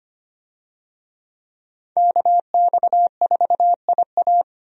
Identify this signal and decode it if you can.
Morse code, heard as a keyed tone, about 25 wpm, 700 Hz. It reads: KX4IA
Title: USA Call Signs - 5 Characters - Repeated 3x - 25wpm